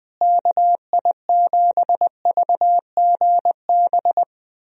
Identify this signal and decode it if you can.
Morse code, heard as a keyed tone, about 20 wpm, 700 Hz. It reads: KI7VGB
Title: USA Call Signs - 6 Characters - Repeated 2x - 20wpm